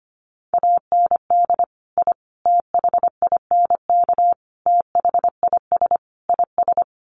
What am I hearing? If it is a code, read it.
Morse code, heard as a keyed tone, about 25 wpm, 700 Hz. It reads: AND I THINK THIS IS
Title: Sets of 5 Words Encore 25wpm